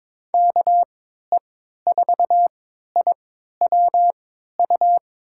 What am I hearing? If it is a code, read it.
Morse code, heard as a keyed tone, about 22 wpm, spaced as for 13 wpm, 700 Hz. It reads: KE4IWU
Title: USA Call Signs - 3x Character Spacing - 22wpm